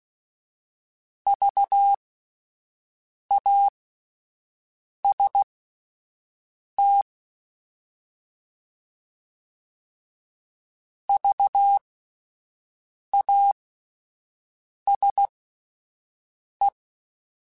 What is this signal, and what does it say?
Morse code, heard as a keyed tone, about 16 wpm, spaced as for 5 wpm, 800 Hz. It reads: VAST VASE